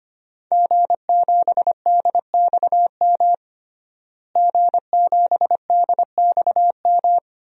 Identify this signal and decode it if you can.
Morse code, heard as a keyed tone, about 25 wpm, 700 Hz. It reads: G7DXM G7DXM
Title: DXCC-100 Call Signs - 5 Characters - Repeated 2x - 25wpm